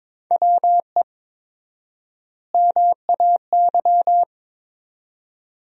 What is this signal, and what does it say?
Morse code, heard as a keyed tone, about 22 wpm, 700 Hz. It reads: WE MAY
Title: 4x Word-Spacing - Sentences from Top 300 Words - 22wpm